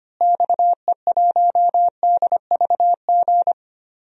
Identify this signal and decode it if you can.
Morse code, heard as a keyed tone, about 25 wpm, 690 Hz. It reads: XE1DVG
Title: DXCC-100 Call Signs - 6 Characters - Repeated 3x - 25wpm